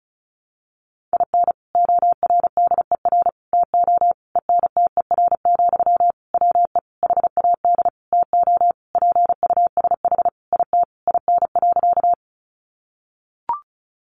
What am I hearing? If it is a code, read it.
Morse code, heard as a keyed tone, about 34 wpm, 700 Hz. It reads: IN ORDER TO ENTER, WE HAD TO PUSH IT IN.
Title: Sentences from Top 1300 Words 35wpm